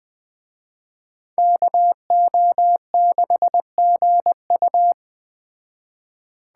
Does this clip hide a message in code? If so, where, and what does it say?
Yes: Morse code from 1.4 to 4.9 s, heard as a keyed tone, about 21 wpm, 700 Hz. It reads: KO6GU